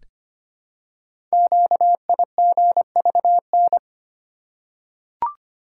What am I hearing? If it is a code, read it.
Morse code, heard as a keyed tone, about 25 wpm, 690 Hz. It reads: QIGVN
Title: Code Groups - 5 Letters - 25wpm